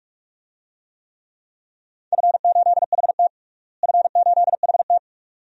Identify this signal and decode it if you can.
Morse code, heard as a keyed tone, about 42 wpm, 700 Hz. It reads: F8HT F8HT
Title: DXCC-100 Call Signs - 4 Characters - Repeated 2x - 45wpm